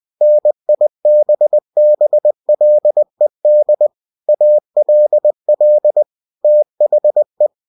Morse code, heard as a keyed tone, about 20 wpm, 600 Hz. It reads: NIBBLED ALL THE
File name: cw-books / winnie-the-pooh / winnie-20wpm-page-0094